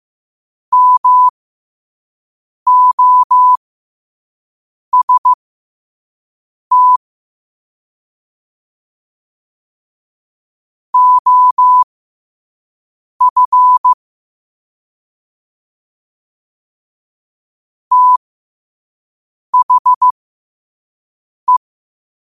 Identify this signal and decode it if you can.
Morse code, heard as a keyed tone, about 15 wpm, spaced as for 5 wpm, 1000 Hz. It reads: MOST OF THE
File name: qotdfiles / 5wpm